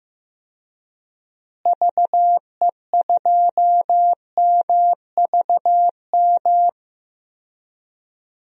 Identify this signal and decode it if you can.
Morse code, heard as a keyed tone, about 15 wpm, 700 Hz. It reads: VE2MVM
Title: DXCC-100 Call Signs - 6 Characters - Repeated 2x - 15wpm